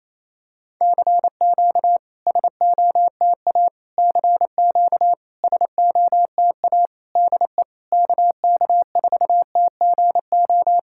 Morse code, heard as a keyed tone, about 28 wpm, 700 Hz. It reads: CQ SOTA CQ SOTA DE KK4TGO
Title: SOTA QSO Practice 28wpm